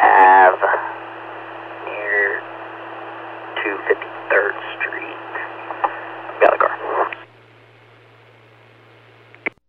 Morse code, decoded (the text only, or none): none